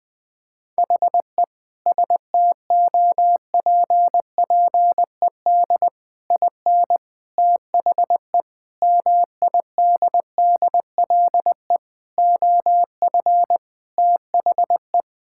Morse code, heard as a keyed tone, about 20 wpm, 700 Hz. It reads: HE STOPPED IN THE MIDDLE OF THE
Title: Sentences with 3-Word Phrases - 20wpm